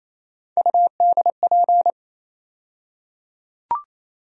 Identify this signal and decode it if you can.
Morse code, heard as a keyed tone, about 28 wpm, 700 Hz. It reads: UDP
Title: USA Call Sign Postfixes - 3 Letters 28wpm